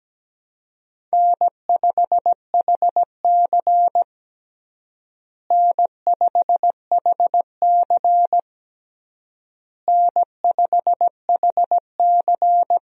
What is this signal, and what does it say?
Morse code, heard as a keyed tone, about 17 wpm, 700 Hz. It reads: N5HC N5HC N5HC
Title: USA Call Signs - 4 Characters - Repeated 3x - 17wpm